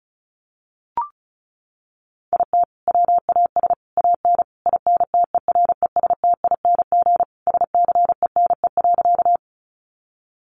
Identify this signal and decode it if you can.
Morse code, heard as a keyed tone, about 34 wpm, 700 Hz. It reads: IT WAS AN INTERESTING SCENE.
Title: Sentences from Top 1100 Words 35wpm